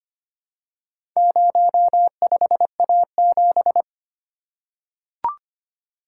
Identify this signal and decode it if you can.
Morse code, heard as a keyed tone, about 25 wpm, 700 Hz. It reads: 05A7
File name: practice / 25wpm / Four Letter-Number Combinations - 25wpm